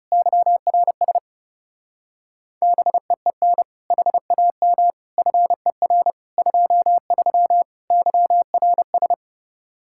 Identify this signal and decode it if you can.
Morse code, heard as a keyed tone, about 37 wpm, 700 Hz. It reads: YRS BEEN HAM FER 23 YRS
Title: Speed-Racing Common QSO Phrases - 30wpm